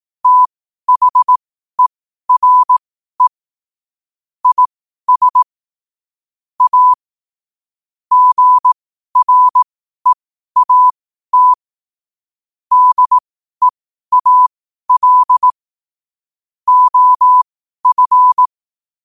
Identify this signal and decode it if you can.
Morse code, heard as a keyed tone, about 18 wpm, spaced as for 12 wpm, 1000 Hz. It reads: THERE IS A GREAT DEAL OF
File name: qotdfiles / 12wpm